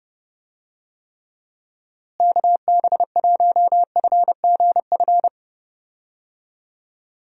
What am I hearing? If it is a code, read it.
Morse code, heard as a keyed tone, about 30 wpm, 700 Hz. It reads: KB1FGF
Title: USA Call Signs - 6 Characters - Repeated 2x - 30wpm